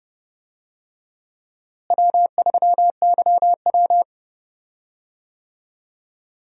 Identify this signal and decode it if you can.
Morse code, heard as a keyed tone, about 32 wpm, 700 Hz. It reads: W3YW
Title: USA Call Signs - 4 Characters - Repeated 3x - 30wpm